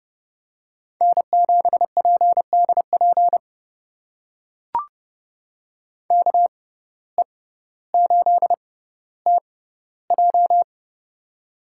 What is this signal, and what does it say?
Morse code, heard as a keyed tone, about 30 wpm, spaced as for 15 wpm, 700 Hz. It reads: N7PDP KE8TJ
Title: USA Call Signs - 5 Characters - 6x Character Spacing - 30wpm